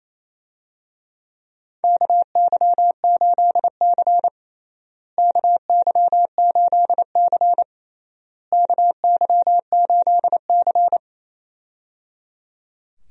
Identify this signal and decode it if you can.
Morse code, heard as a keyed tone, about 28 wpm, 700 Hz. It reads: KY8C KY8C KY8C
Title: USA Call Signs - 4 Characters - Repeated 3x - 28wpm